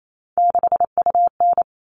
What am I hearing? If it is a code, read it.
Morse code, heard as a keyed tone, about 28 wpm, 700 Hz. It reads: 6UN
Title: All Cut Numbers - 3-Digit 28wpm